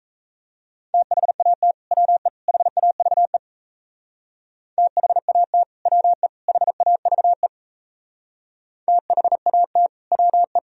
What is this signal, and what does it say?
Morse code, heard as a keyed tone, about 35 wpm, 700 Hz. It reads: THAT WE HAVE THAT WE HAVE THAT WE
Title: Speed-Racing - Sets of 3 Words 28wpm